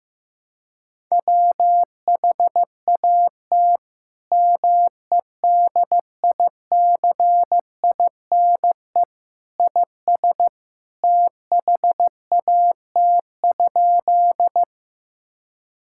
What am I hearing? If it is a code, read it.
Morse code, heard as a keyed tone, about 15 wpm, 700 Hz. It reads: WHAT MEDICINE IS THAT?
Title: Sentences from Top 1000 Words - Review Part 6 - 15wpm